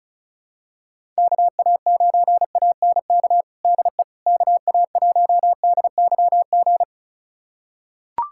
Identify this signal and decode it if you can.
Morse code, heard as a keyed tone, about 34 wpm, 700 Hz. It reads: KA9ANK DE KA1DYG
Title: USA Callsign DE Callsign - 6 Characters - 35wpm